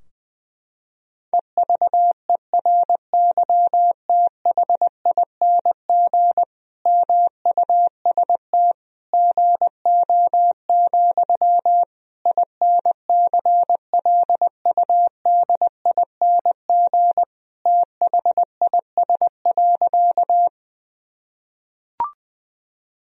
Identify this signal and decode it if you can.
Morse code, heard as a keyed tone, about 21 wpm, 700 Hz. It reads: EVERYTHING MUST GO, INCLUDING THIS.
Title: Sentences from Top 1000 Words - Review Part 4 - 20wpm